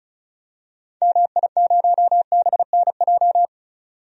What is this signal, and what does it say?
Morse code, heard as a keyed tone, about 34 wpm, 700 Hz. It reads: MI0BNJ
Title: DXCC-100 Call Signs - 6 Characters - 35wpm